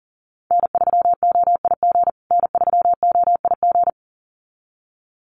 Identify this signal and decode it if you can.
Morse code, heard as a keyed tone, about 40 wpm, 700 Hz. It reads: N3OIG N3OIG
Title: USA Call Signs Repeated 2x 40wpm